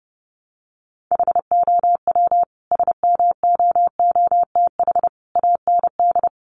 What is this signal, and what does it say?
Morse code, heard as a keyed tone, about 28 wpm, 700 Hz. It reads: HOW SMOOTH AND
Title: Sentences from Top 1400 Words 30wpm